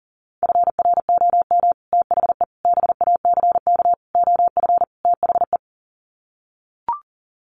Morse code, heard as a keyed tone, about 40 wpm, 700 Hz. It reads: FROM THE BACK OF THE
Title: Sets of 5 Words Encore 40wpm